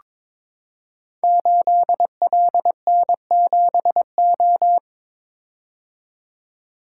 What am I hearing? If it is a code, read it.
Morse code, heard as a keyed tone, about 22 wpm, 700 Hz. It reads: 8LN7O
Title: Code Groups - 5 Letter:Numbers - 22wpm